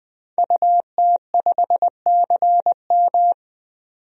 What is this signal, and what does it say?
Morse code, heard as a keyed tone, about 20 wpm, 700 Hz. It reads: UT5CM
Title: DXCC-100 Call Signs - 5 Characters - 20wpm